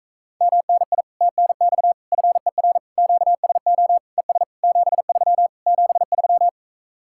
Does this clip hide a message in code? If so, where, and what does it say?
Yes: Morse code from 0.4 to 6.5 s, heard as a keyed tone, about 45 wpm, 700 Hz. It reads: MNI TNX FER QSO ES 73 73